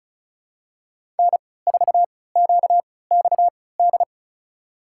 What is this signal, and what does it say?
Morse code, heard as a keyed tone, about 34 wpm, spaced as for 20 wpm, 700 Hz. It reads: N4QXD
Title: USA Call Signs - 5 Characters - 3x Character Spacing - 35wpm